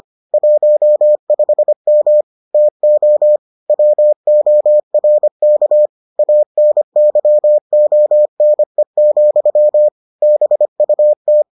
Morse code, heard as a keyed tone, about 25 wpm, 600 Hz. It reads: 15M TO WORK ANYONE, BUT